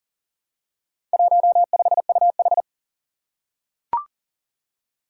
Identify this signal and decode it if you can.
Morse code, heard as a keyed tone, about 40 wpm, 700 Hz. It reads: 15UH
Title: Four Challenging Letter-Number Combinations 40wpm